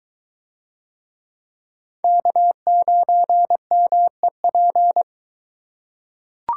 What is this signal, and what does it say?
Morse code, heard as a keyed tone, about 21 wpm, 700 Hz. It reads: K9MEP